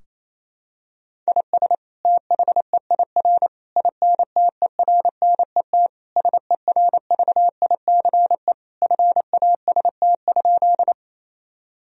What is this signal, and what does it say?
Morse code, heard as a keyed tone, about 28 wpm, 700 Hz. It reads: IS THEIR INTERNET SERVICE FAST?
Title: Sentences from Top 1000 Words - Review Part 4 - 28wpm